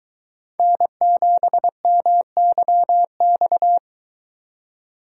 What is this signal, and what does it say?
Morse code, heard as a keyed tone, about 20 wpm, 700 Hz. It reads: N7MYX